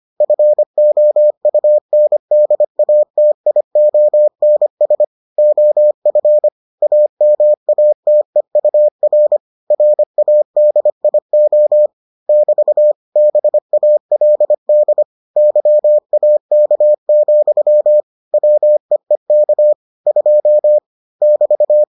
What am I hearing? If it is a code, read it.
Morse code, heard as a keyed tone, about 25 wpm, 600 Hz. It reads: FOUNDATIONS OF AMATEUR RADIO <BT> BALD YAK, WEEK 2 <BT>